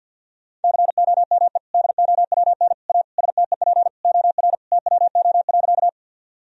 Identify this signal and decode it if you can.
Morse code, heard as a keyed tone, about 50 wpm, 700 Hz. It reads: COME DOWN A STEP OR TWO.
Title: Sentences from Top 1000 Words - Review Part 2 - 50wpm